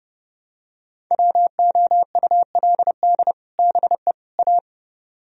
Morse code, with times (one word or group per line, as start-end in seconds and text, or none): 1.1-3.3 WOULD
3.6-4.1 BE
4.4-4.6 A